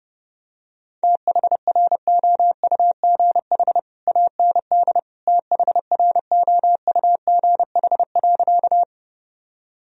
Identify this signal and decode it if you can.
Morse code, heard as a keyed tone, about 32 wpm, 700 Hz. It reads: THROUGH AND THROUGH.
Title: Sentences from Top 1000 Words - Review Part 1 - 30wpm